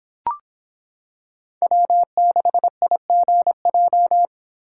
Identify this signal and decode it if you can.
Morse code, heard as a keyed tone, about 26 wpm, 700 Hz. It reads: W6IGJ